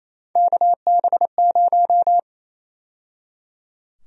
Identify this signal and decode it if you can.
Morse code, heard as a keyed tone, about 28 wpm, 700 Hz. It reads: KB0
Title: USA Call Sign Prefixes - 28wpm